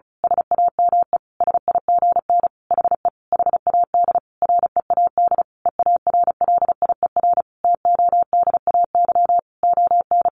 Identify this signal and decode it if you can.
Morse code, heard as a keyed tone, about 34 wpm, 700 Hz. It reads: SAME SIGN HE HAD READ EARLIER TODAY ON
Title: A Sound Of Thunder 35wpm